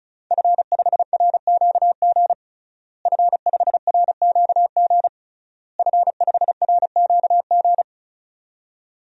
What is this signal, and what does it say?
Morse code, heard as a keyed tone, about 34 wpm, 700 Hz. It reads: F5RQG F5RQG F5RQG
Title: DXCC-100 Call Signs - 5 Characters - Repeated 3x - 35wpm